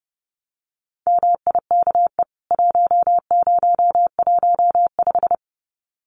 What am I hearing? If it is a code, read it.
Morse code, heard as a keyed tone, about 30 wpm, 700 Hz. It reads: MIKE 1015
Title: CWT Exchange 30wpm